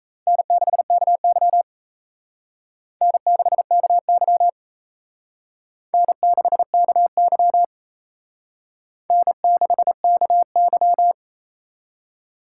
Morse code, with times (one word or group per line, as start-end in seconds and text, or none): 0.3-1.6 N6KY
3.0-4.5 N6KY
5.9-7.6 N6KY
9.1-11.1 N6KY